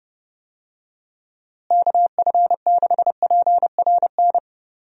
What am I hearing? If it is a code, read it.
Morse code, heard as a keyed tone, about 32 wpm, 700 Hz. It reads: KF6PRN
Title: USA Call Signs - 6 Characters - Repeated 2x - 30wpm